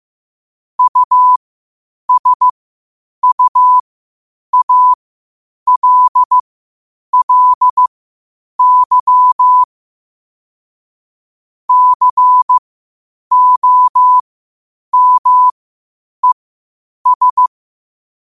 Morse code, heard as a keyed tone, about 15 wpm, spaced as for 8 wpm, 1000 Hz. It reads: USUALLY COMES
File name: qotdfiles / 8wpm